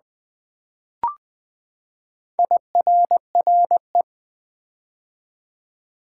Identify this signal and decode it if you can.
Morse code, heard as a keyed tone, about 20 wpm, 700 Hz. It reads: IRRE
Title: Build A Word - 4 Letters At A Time - 10-Letter Words - 20wpm